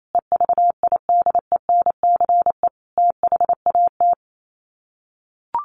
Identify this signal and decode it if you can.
Morse code, heard as a keyed tone, about 28 wpm, 700 Hz. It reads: EVIDENCE THAT
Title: Sets of 5 Words Encore 28wpm